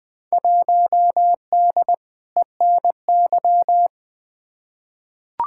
Morse code, heard as a keyed tone, about 20 wpm, 700 Hz. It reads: 1D ENY
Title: Field Day Exchanges Encore - 20wpm